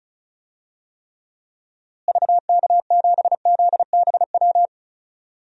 Morse code, heard as a keyed tone, about 36 wpm, 700 Hz. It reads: VK7ZBW
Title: DXCC-100 Call Signs - 6 Characters - Repeated 2x - 35wpm